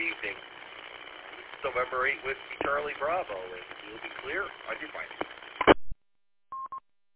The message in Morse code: none

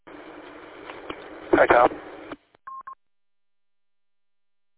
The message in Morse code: none

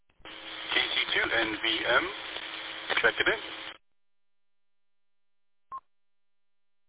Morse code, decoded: none

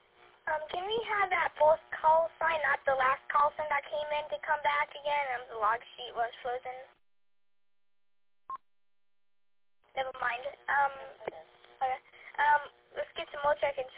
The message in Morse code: none

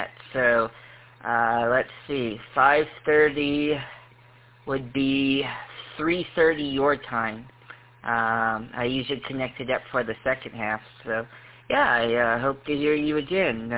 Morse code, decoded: none

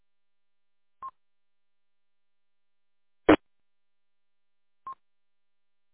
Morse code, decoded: none

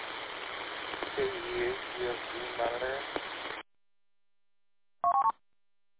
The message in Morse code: none

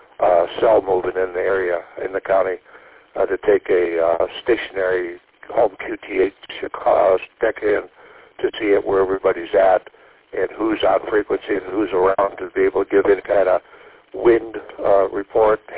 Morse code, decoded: none

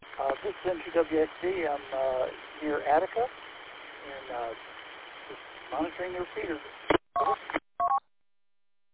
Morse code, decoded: none

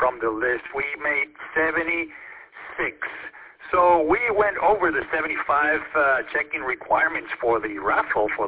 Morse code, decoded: none

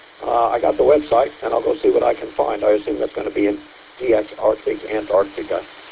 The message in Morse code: none

